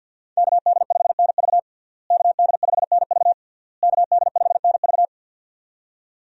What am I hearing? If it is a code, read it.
Morse code, heard as a keyed tone, about 52 wpm, 700 Hz. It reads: KD5NV KD5NV KD5NV
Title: USA Call Signs - 5 Characters - Repeated 3x - 50wpm